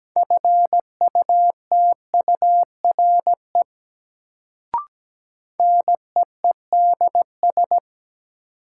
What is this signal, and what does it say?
Morse code, heard as a keyed tone, about 17 wpm, 700 Hz. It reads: FUTURE NEEDS
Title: Sets of 3 Words - New Word Drill - 17wpm